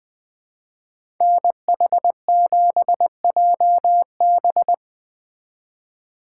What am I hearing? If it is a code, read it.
Morse code, heard as a keyed tone, about 20 wpm, 700 Hz. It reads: NH7JB
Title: Speed Racing - USA 2x2 Call Signs 17wpm